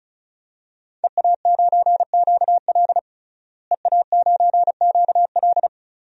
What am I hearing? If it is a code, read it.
Morse code, heard as a keyed tone, about 35 wpm, 700 Hz. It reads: EA9QL EA9QL
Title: DXCC-100 Call Signs - 5 Characters - Repeated 3x - 35wpm